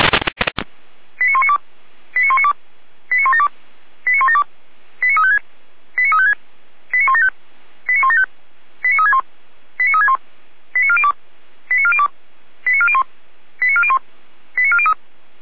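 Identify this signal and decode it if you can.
Five-tone selective call (ZVEI), heard as quick runs of five tones, tones of about 70 ms. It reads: 89192 89192 89172 89172 89356 89356 89175 89175 89361 89361 89402 89402 89401 89401 89403
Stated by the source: ZVEI five-tone call received on SDR.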